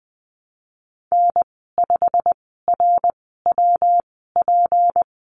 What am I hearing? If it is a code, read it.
Morse code, heard as a keyed tone, about 20 wpm, spaced as for 15 wpm, 700 Hz. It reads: N5RWP
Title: 173 Lesson 28g - Number 9 - Call Signs 20wpm